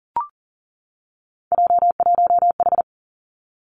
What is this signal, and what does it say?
Morse code, heard as a keyed tone, about 42 wpm, 700 Hz. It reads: J1H
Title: Three Challenging Letter-Number Combinations 40wpm